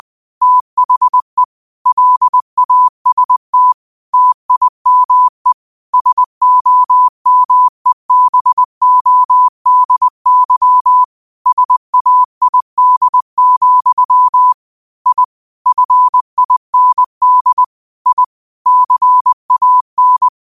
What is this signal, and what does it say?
Morse code, heard as a keyed tone, about 20 wpm, 1000 Hz. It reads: THE LAST TIME SOMEBODY SAID, I FIND I CAN
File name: qotdfiles / 20wpm